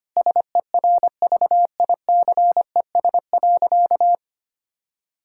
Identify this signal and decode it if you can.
Morse code, heard as a keyed tone, about 25 wpm, 700 Hz. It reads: SERVICES.
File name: practice / 25wpm / Sentences with 4-Word Phrases - 25wpm